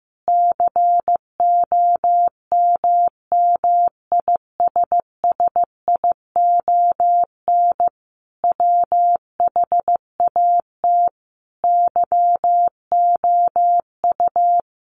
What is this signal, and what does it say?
Morse code, heard as a keyed tone, about 15 wpm, 700 Hz. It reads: COMMISSION WHAT YOU
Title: Sentences from Top 1400 Words 15wpm